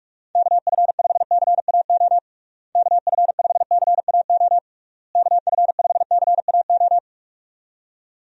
Morse code, heard as a keyed tone, about 41 wpm, 700 Hz. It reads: KF5CAO KF5CAO KF5CAO
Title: USA Call Signs - 6 Characters - Repeated 3x - 45wpm